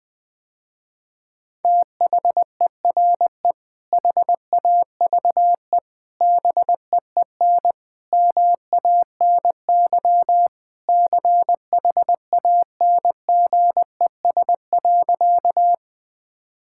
Morse code, heard as a keyed tone, about 20 wpm, 700 Hz. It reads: THERE HAVE BEEN MANY CHANGES.
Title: Sentences with 3-Word Phrases - Spoken Beginning - 20wpm